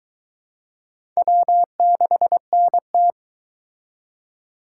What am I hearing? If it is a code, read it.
Morse code, heard as a keyed tone, about 21 wpm, 700 Hz. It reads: W6NT